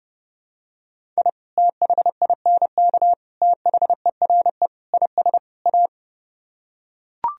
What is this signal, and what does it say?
Morse code, heard as a keyed tone, about 30 wpm, 700 Hz. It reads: I THINK THERE IS A